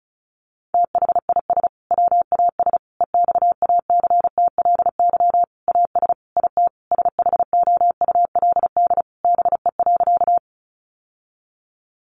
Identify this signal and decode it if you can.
Morse code, heard as a keyed tone, about 34 wpm, 700 Hz. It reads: THIS WAS EXACTLY AS IT SHOULD BE.
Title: Sentences from Top 900 Words 35wpm